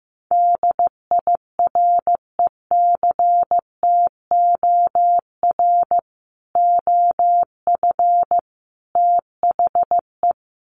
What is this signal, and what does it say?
Morse code, heard as a keyed tone, about 15 wpm, 700 Hz. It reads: DIRECTOR OF THE
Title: Sets of 3 Words 15wpm